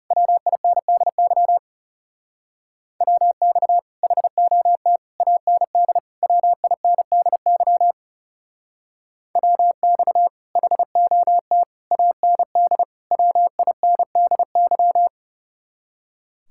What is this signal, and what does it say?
Morse code, heard as a keyed tone, about 40 wpm, 700 Hz. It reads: WINDY WX HOT AND WINDY WX HOT AND WINDY